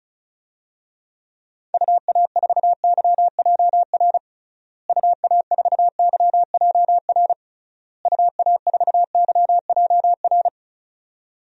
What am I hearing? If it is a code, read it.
Morse code, heard as a keyed tone, about 36 wpm, 700 Hz. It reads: UA4YJR UA4YJR UA4YJR
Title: DXCC-100 Call Signs - 6 Characters - Repeated 3x - 35wpm